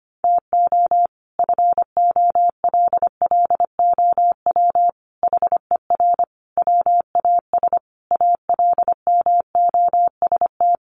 Morse code, heard as a keyed tone, about 25 wpm, 700 Hz. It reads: TO FOLLOW HER WAS ALMOST
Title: Sentences from Top 1100 Words 25wpm